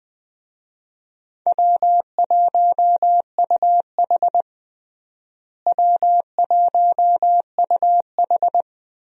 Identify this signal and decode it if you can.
Morse code, heard as a keyed tone, about 20 wpm, 700 Hz. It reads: W1UH W1UH
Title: USA Call Signs - 4 Characters - Repeated 2x - 20wpm